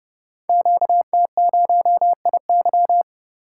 Morse code, heard as a keyed tone, about 30 wpm, 700 Hz. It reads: QT0IY
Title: Code Groups - 5 Letter:Numbers - 30wpm